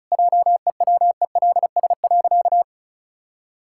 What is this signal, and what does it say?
Morse code, heard as a keyed tone, about 36 wpm, 700 Hz. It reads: JEWELS.
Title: Pangram Challenge v2 - 35wpm